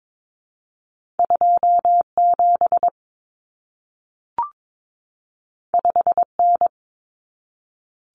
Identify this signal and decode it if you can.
Morse code, heard as a keyed tone, about 22 wpm, 700 Hz. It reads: 27 5N
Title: Common Cut Numbers - 2-Digit 22wpm